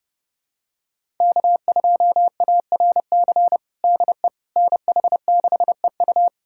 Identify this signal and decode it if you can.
Morse code, heard as a keyed tone, about 32 wpm, 700 Hz. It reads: K2ARC DE NH6EU
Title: USA Callsign DE Callsign - 5 Characters - 30wpm